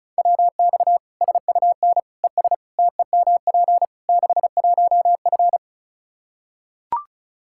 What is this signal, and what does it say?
Morse code, heard as a keyed tone, about 36 wpm, 700 Hz. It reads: WX SUN ES TEMP 61F
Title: Practice QSO - Exchange 2 - Info - 35wpm